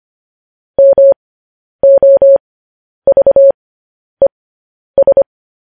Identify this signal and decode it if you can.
Morse code, heard as a keyed tone, about 25 wpm, spaced as for 10 wpm, 550 Hz. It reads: MOVES
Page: news-headlines-in-morse-code-at-10-wpm